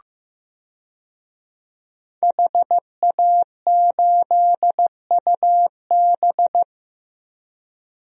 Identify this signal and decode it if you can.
Morse code, heard as a keyed tone, about 15 wpm, 700 Hz. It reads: HA8UB
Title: DXCC-100 Call Signs - 5 Characters - Repeated 3x - 15wpm